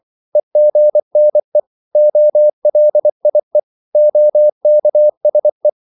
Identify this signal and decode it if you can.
Morse code, heard as a keyed tone, about 24 wpm, 600 Hz. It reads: EGNE OLIE OKSE
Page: Korte danske ord 24wpm